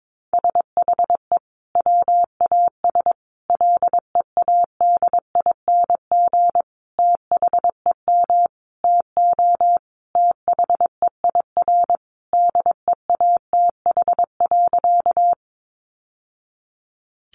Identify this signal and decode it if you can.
Morse code, heard as a keyed tone, about 22 wpm, 700 Hz. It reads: SHE WAS LEADING THEM TO THEIR DEATH.
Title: Sentences from Top 1100 Words 22wpm